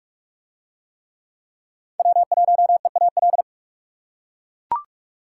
Morse code, heard as a keyed tone, about 39 wpm, 700 Hz. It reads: W1EAL